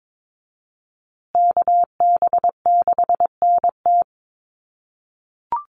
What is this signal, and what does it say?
Morse code, heard as a keyed tone, about 22 wpm, 700 Hz. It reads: KB6NT
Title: USA Call Signs Repeated 3x 22wpm